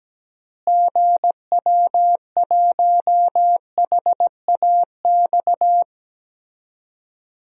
Morse code, heard as a keyed tone, about 17 wpm, 700 Hz. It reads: GW1HAX
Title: DXCC-100 Call Signs - 6 Characters - Repeated 3x - 17wpm